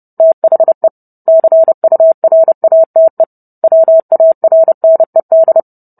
Morse code, heard as a keyed tone, about 30 wpm, 650 Hz. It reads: THE CURATE WARNED